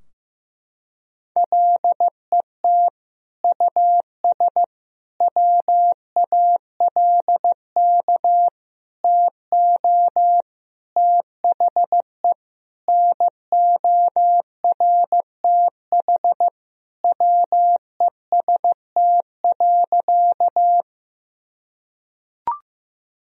Morse code, heard as a keyed tone, about 15 wpm, 700 Hz. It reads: LET US WALK TO THE NORTH WEST.
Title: Sentences from Top 1000 Words - Review Part 2 - 15wpm